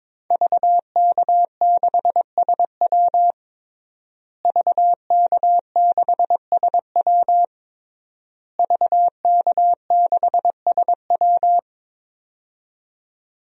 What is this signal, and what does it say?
Morse code, heard as a keyed tone, about 22 wpm, 700 Hz. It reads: VK6SW VK6SW VK6SW
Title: DXCC-100 Call Signs - 5 Characters - Repeated 3x - 22wpm